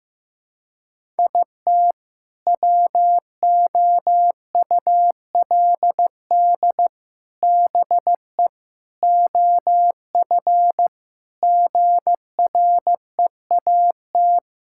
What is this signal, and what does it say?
Morse code, heard as a keyed tone, about 16 wpm, 700 Hz. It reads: IT WOULD BE OF GREAT
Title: Sentences from Top 1000 Words - Review Part 6 - 15wpm